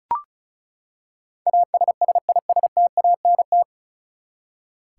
Morse code, heard as a keyed tone, about 34 wpm, 700 Hz. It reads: ASSISTANT
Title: Practice QSO - Exchange 3 - More Info - 35wpm